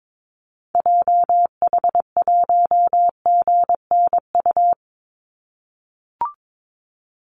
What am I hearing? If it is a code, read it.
Morse code, heard as a keyed tone, about 22 wpm, 700 Hz. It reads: JH1GNU
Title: Call Signs - 22wpm